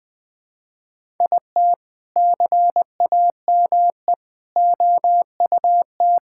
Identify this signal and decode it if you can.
Morse code, heard as a keyed tone, about 21 wpm, 700 Hz. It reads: IT CAME OUT
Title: Sentences from Top 1000 Words - Review Part 5 - 20wpm